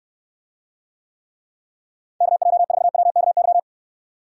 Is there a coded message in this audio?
Yes: Morse code from 2.2 to 3.6 s, heard as a keyed tone, about 69 wpm, 700 Hz. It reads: KJ4WKY